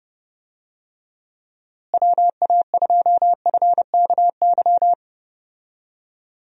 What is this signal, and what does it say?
Morse code, heard as a keyed tone, about 30 wpm, 700 Hz. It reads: WA2FKY
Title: USA Call Signs - 6 Characters - Repeated 3x - 30wpm